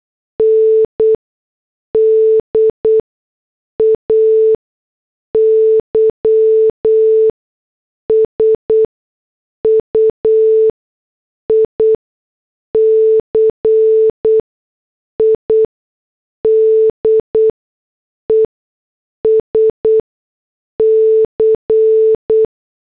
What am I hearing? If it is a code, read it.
Morse code, heard as a keyed tone, about 8 wpm, spaced as for 6 wpm, 440 Hz. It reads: N DAY SUICIDES C